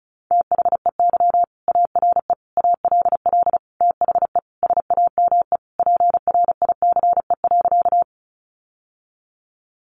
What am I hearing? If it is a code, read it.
Morse code, heard as a keyed tone, about 34 wpm, 700 Hz. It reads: THEY ARE ALL THE SAME PRICE.
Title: Sentences from Top 600 Words 35wpm